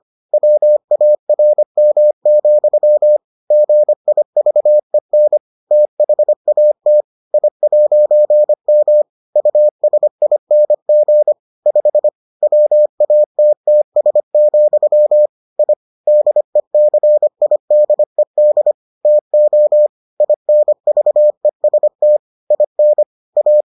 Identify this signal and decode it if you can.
Morse code, heard as a keyed tone, about 25 wpm, 600 Hz. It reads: WARM, GIVEN THAT I'M USING 5 WATTS, I DECIDED TO INVEST IN A